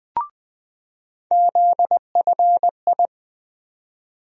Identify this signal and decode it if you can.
Morse code, heard as a keyed tone, about 20 wpm, 700 Hz. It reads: ZFI
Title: USA Call Sign Postfixes - 3 Letters 20wpm